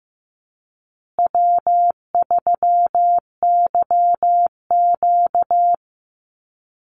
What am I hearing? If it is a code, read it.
Morse code, heard as a keyed tone, about 15 wpm, 700 Hz. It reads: W3YQ
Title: Easy Call Signs Repeated 3x - 15wpm